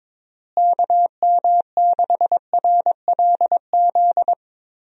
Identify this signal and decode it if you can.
Morse code, heard as a keyed tone, about 22 wpm, 700 Hz. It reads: KM6RLZ
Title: USA Call Signs - 6 Characters - Repeated 3x - 22wpm